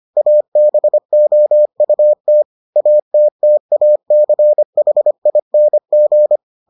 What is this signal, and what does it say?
Morse code, heard as a keyed tone, about 25 wpm, 600 Hz. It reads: ABOUT ATTACHING